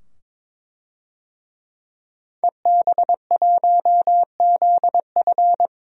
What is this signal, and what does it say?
Morse code, heard as a keyed tone, about 22 wpm, 700 Hz. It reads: EB1ZF
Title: DXCC-100 Call Signs - 5 Characters - 22wpm